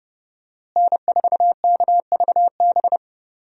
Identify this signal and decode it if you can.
Morse code, heard as a keyed tone, about 31 wpm, 700 Hz. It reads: N4KVB
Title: USA Call Signs - 5 Characters - Repeated 3x - 30wpm